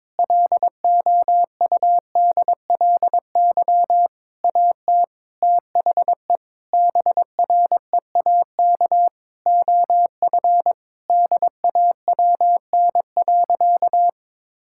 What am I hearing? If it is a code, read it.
Morse code, heard as a keyed tone, about 22 wpm, 700 Hz. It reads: LOUDLY AT THE BREAK OF DAWN.
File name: Difficult Sentences with Hints - 22wpm